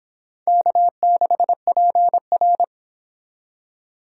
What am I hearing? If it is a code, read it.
Morse code, heard as a keyed tone, about 30 wpm, 700 Hz. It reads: K6PR